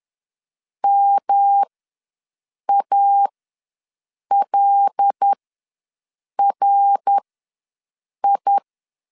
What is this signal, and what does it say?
Morse code, heard as a keyed tone, about 10 wpm, spaced as for 6 wpm, 800 Hz. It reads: MALRI
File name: LEZIONI / 5 LBVC / 5 LBV ws 10wpm 3645